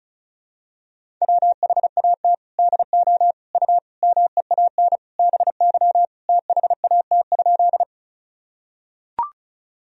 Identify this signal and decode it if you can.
Morse code, heard as a keyed tone, about 36 wpm, 700 Hz. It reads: WHAT DO U MEAN BY THAT?
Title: Sentences from Top 200 Words and Abbreviations 35wpm